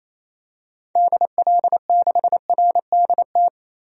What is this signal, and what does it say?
Morse code, heard as a keyed tone, about 28 wpm, 700 Hz. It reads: DL6RDT
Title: DXCC-100 Call Signs - 6 Characters - 28wpm